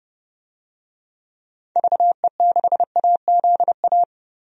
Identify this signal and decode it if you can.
Morse code, heard as a keyed tone, about 32 wpm, 700 Hz. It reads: VE6AZA
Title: DXCC-100 Call Signs - 6 Characters - Repeated 3x - 30wpm